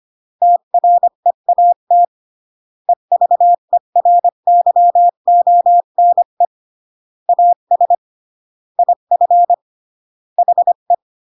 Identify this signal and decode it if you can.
Morse code, heard as a keyed tone, about 25 wpm, 700 Hz. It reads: TREAT EVERYONE AS IF HE